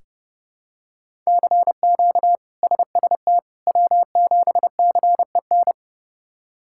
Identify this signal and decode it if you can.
Morse code, heard as a keyed tone, about 30 wpm, 700 Hz. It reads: CQ SST W7CEN
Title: K1USN SST QSO Practice - 30wpm